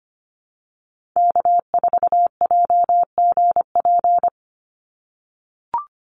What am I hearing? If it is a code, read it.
Morse code, heard as a keyed tone, about 25 wpm, 700 Hz. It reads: K4JGP
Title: USA Call Signs 25wpm